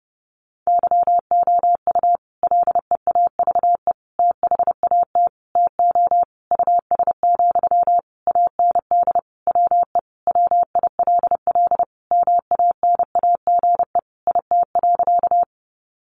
Morse code, heard as a keyed tone, about 28 wpm, 700 Hz. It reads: YOU LEAVE THAT TO US, AND WE WILL MANAGE IT.
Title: Sentences from Top 1400 Words 30wpm